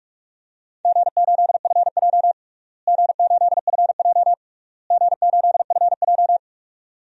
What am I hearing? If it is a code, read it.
Morse code, heard as a keyed tone, about 46 wpm, 700 Hz. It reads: G8FJ G8FJ G8FJ
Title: DXCC-100 Call Signs - 4 Characters - Repeated 3x - 45wpm